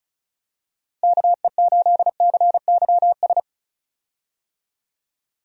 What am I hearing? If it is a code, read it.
Morse code, heard as a keyed tone, about 36 wpm, 700 Hz. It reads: KE8CYS
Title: USA Call Signs - 6 Characters - Repeated 2x - 35wpm